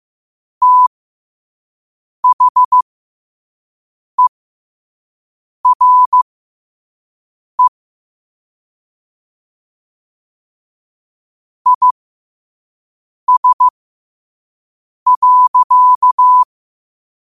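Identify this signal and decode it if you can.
Morse code, heard as a keyed tone, about 15 wpm, spaced as for 5 wpm, 1000 Hz. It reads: THERE IS.
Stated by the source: Morse quote of the day at 5 wpm.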